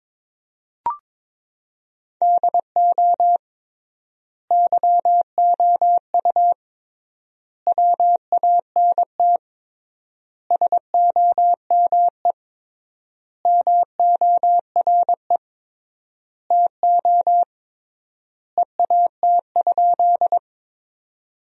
Morse code, heard as a keyed tone, about 22 wpm, 700 Hz. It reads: DO YOU WANT SOME MORE TO EAT?
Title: 3x Word-Spacing - Sentences from Top 300 Words - 22wpm